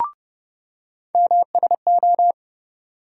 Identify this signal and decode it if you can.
Morse code, heard as a keyed tone, about 30 wpm, 700 Hz. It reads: MSO
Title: USA Call Sign Postfixes - 3 Letters 30wpm